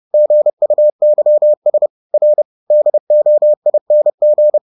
Morse code, heard as a keyed tone, about 30 wpm, 600 Hz. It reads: GUYS R DOING